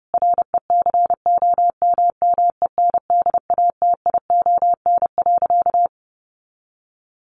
Morse code, heard as a keyed tone, about 30 wpm, 700 Hz. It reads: RECOMMENDATION.
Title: Sentences from Top 1400 Words 30wpm